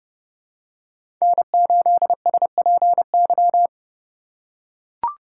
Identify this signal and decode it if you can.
Morse code, heard as a keyed tone, about 30 wpm, 700 Hz. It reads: N8SPY
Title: USA Call Signs - 5 Characters - 3x Character Spacing - 30wpm